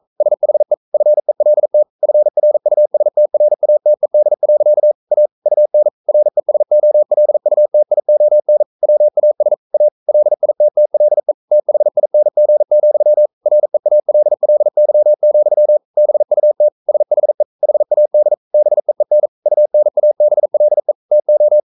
Morse code, heard as a keyed tone, about 42 wpm, 600 Hz. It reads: SHE FELT FRUSTRATED. A UN RESOLUTION WAS A LITTLE THING, REALLY, BUT SHE HAD BEEN UNABLE TO
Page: Romanen Never af Ken Follet på engelsk.